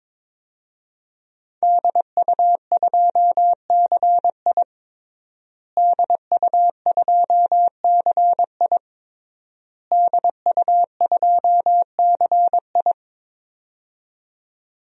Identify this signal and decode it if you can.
Morse code, heard as a keyed tone, about 22 wpm, 700 Hz. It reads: DU2CI DU2CI DU2CI
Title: DXCC-100 Call Signs - 5 Characters - Repeated 3x - 22wpm